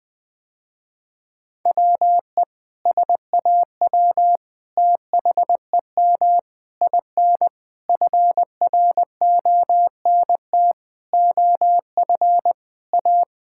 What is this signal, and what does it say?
Morse code, heard as a keyed tone, about 20 wpm, 700 Hz. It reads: WE SAW THEM IN FRONT OF A
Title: Sentences with 4-Word Phrases - Spoken Beginning - 20wpm